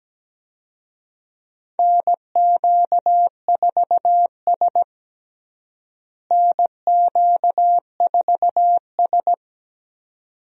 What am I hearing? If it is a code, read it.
Morse code, heard as a keyed tone, about 17 wpm, 700 Hz. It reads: NQ4S NQ4S
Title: USA Call Signs - 4 Characters - Repeated 2x - 17wpm